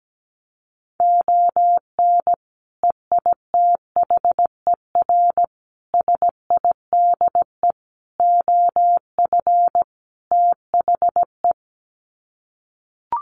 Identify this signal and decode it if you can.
Morse code, heard as a keyed tone, about 17 wpm, 700 Hz. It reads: ON EITHER SIDE OF THE
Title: Sets of 5 Words 17wpm